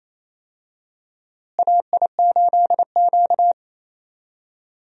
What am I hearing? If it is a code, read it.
Morse code, heard as a keyed tone, about 28 wpm, 700 Hz. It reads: AI8Q
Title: USA Call Signs - 4 Characters - Repeated 3x - 28wpm